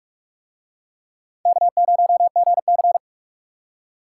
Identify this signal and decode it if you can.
Morse code, heard as a keyed tone, about 43 wpm, 700 Hz. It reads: K0GC